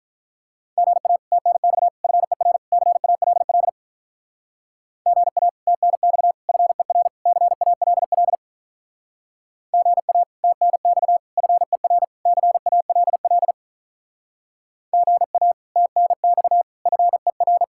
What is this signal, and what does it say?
Morse code, heard as a keyed tone, about 43 wpm, 700 Hz. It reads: GA TNX FER CALL GA TNX FER CALL GA TNX FER CALL GA TNX FER
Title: Speed-Racing Common QSO Phrases - 35wpm